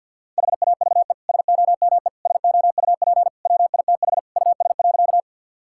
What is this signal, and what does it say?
Morse code, heard as a keyed tone, about 50 wpm, 700 Hz. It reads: HAVE SOME SOUP WITH US.
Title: Sentences from Top 1000 Words - Review Part 6 - 50wpm